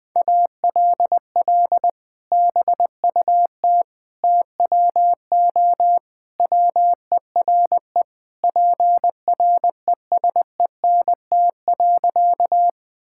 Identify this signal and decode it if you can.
Morse code, heard as a keyed tone, about 20 wpm, 700 Hz. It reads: ALL BUT TWO WERE PRESENT.
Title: Sentences from Top 1000 Words - Review Part 5 - 20wpm